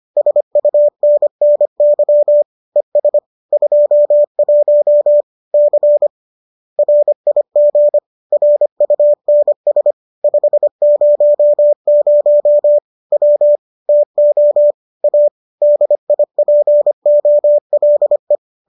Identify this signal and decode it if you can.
Morse code, heard as a keyed tone, about 25 wpm, 590 Hz. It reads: SUNNY ES 21 C RIG RUNS 500 W TO A DIPOLE